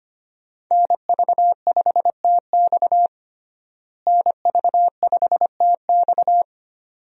Morse code, heard as a keyed tone, about 25 wpm, 700 Hz. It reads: NV5TX NV5TX
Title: USA Call Signs - 5 Characters - Repeated 2x - 25wpm